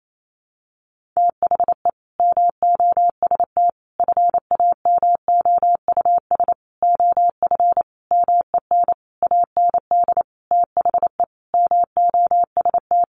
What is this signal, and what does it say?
Morse code, heard as a keyed tone, about 28 wpm, 700 Hz. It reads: THE MOST FAMOUS OF MEN AND THE MOST
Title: Sentences from Top 1500 Words 28wpm